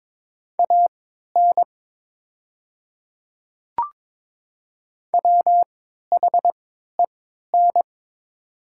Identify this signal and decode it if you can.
Morse code, heard as a keyed tone, about 22 wpm, spaced as for 12 wpm, 700 Hz. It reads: AN WHEN
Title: 3x Character Spacing - Top 300 Words - 22wpm